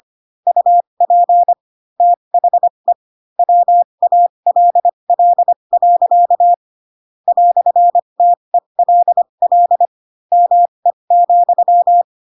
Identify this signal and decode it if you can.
Morse code, heard as a keyed tone, about 25 wpm, spaced as for 22 wpm, 700 Hz. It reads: UP THE WALL. "TELL ME,
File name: Pinocchio / Pinocchio ch4 18 25 ik1hge